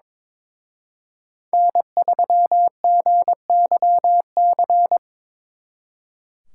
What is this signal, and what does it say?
Morse code, heard as a keyed tone, about 25 wpm, 700 Hz. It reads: N3GYC
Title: Speed Racing - USA 1x3 Call Signs 22wpm